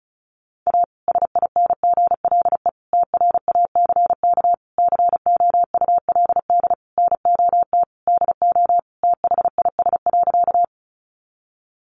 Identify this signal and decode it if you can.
Morse code, heard as a keyed tone, about 34 wpm, 700 Hz. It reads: A SINGLE TRACK COULD NOT DO THIS.
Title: Sentences from Top 1100 Words 35wpm